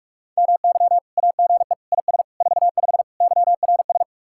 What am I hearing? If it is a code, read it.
Morse code, heard as a keyed tone, about 56 wpm, 700 Hz. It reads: MY AGE IS 45 YRS